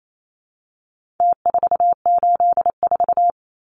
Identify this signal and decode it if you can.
Morse code, heard as a keyed tone, about 28 wpm, 700 Hz. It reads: T484
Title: Numbers - Contest Style - 4-Digit 28wpm